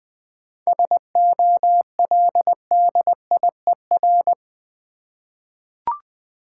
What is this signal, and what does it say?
Morse code, heard as a keyed tone, about 20 wpm, 700 Hz. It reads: SOLDIER
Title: 7 Letter Words - First 3 Letters Given - 20wpm